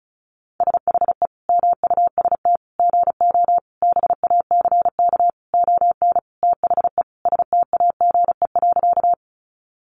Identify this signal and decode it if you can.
Morse code, heard as a keyed tone, about 34 wpm, 700 Hz. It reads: SHE MUST GO BACK ON THE STAGE.
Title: Sentences from Top 1200 Words 35wpm